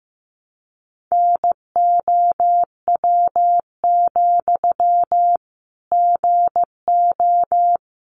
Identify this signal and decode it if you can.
Morse code, heard as a keyed tone, about 15 wpm, 700 Hz. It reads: NOW, GO
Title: Sentences from Top 200 Words 15wpm